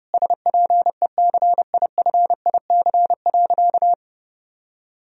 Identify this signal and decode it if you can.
Morse code, heard as a keyed tone, about 30 wpm, 700 Hz. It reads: SPECIFIC.
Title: Sentences from Top 1000 Words - Review Part 3 - 30wpm